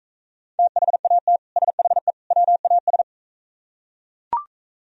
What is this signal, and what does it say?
Morse code, heard as a keyed tone, about 45 wpm, 700 Hz. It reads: THAT SHE WAS